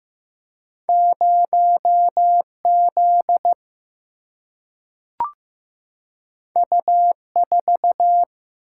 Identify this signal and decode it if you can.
Morse code, heard as a keyed tone, about 15 wpm, 700 Hz. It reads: 0 Z U 4
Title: Single-Letter Single-Number 15wpm